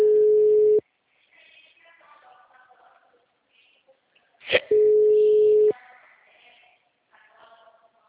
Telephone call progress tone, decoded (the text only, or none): ringback tone